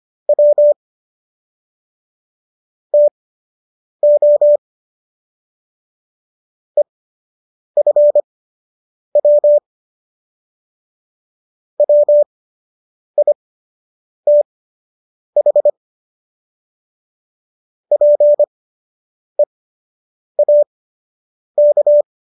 Morse code, heard as a keyed tone, about 25 wpm, spaced as for 8 wpm, 590 Hz. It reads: W TO EFW WITH PEAK